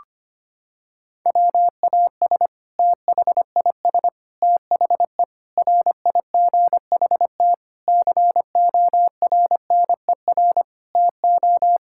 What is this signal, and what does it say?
Morse code, heard as a keyed tone, about 25 wpm, 700 Hz. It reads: WAS THIS THE RIGHT CORNER TO
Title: Sentences from Top 1600 Words 25wpm